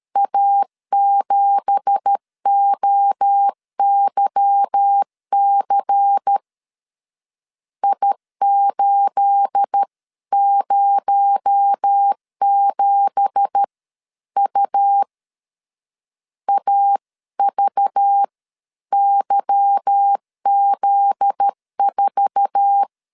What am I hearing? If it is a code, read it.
Morse code, heard as a keyed tone, about 13 wpm, spaced as for 11 wpm, 800 Hz. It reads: A7OYC I807U AVYZ4